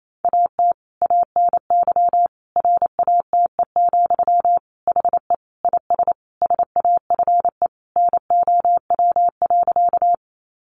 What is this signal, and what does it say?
Morse code, heard as a keyed tone, about 28 wpm, 700 Hz. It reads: AT ANY RATE, HE IS SAFE NOW.
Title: Sentences from Top 900 Words 28wpm